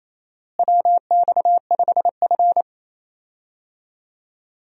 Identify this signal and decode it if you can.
Morse code, heard as a keyed tone, about 27 wpm, 700 Hz. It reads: WX5F
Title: USA Call Signs - 4 Characters - Repeated 3x - 28wpm